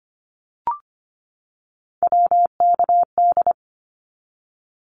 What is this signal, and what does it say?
Morse code, heard as a keyed tone, about 25 wpm, 700 Hz. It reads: WKD
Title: Abbreviations 25wpm